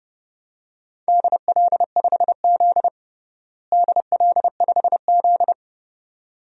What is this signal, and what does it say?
Morse code, heard as a keyed tone, about 32 wpm, 700 Hz. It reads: DL5Z DL5Z
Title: DXCC-100 Call Signs - 4 Characters - Repeated 2x - 30wpm